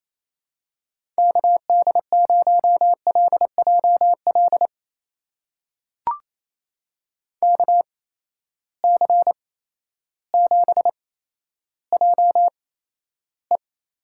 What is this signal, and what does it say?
Morse code, heard as a keyed tone, about 28 wpm, spaced as for 13 wpm, 700 Hz. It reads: KD0LJL KC7JE
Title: USA Call Signs - 6 Characters - 8x Character Spacing - 28wpm